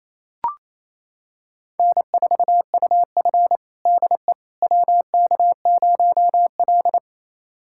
Morse code, heard as a keyed tone, about 28 wpm, 700 Hz. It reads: N4UF DE WK0L
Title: USA Callsign DE Callsign - 4 Characters - 28wpm